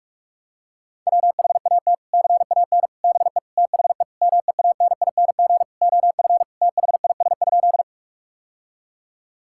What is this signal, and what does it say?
Morse code, heard as a keyed tone, about 41 wpm, 700 Hz. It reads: WHAT CAN BE THE MEANING OF THIS?
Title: Sentences from Top 1000 Words - Review Part 6 - 45wpm